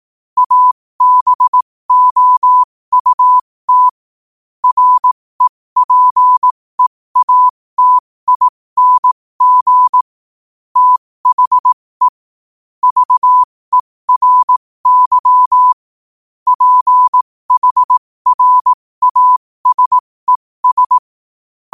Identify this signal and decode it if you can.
Morse code, heard as a keyed tone, about 18 wpm, spaced as for 15 wpm, 1000 Hz. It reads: ABOUT REPEATING THE VERY PHRASES
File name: qotdfiles / 15wpm